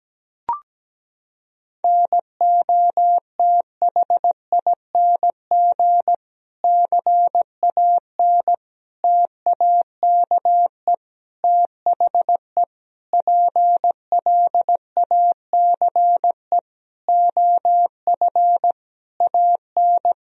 Morse code, heard as a keyed tone, about 17 wpm, 700 Hz. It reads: NOTHING CAN TAKE THE PLACE OF AN
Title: Sentences from Top 1000 Words - Review Part 4 - 17wpm